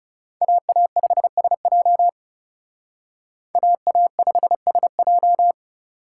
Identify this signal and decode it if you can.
Morse code, heard as a keyed tone, about 38 wpm, 700 Hz. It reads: AA5SJ AA5SJ